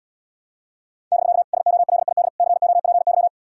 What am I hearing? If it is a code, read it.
Morse code, heard as a keyed tone, about 80 wpm, 700 Hz. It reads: <BK> SOLID COPY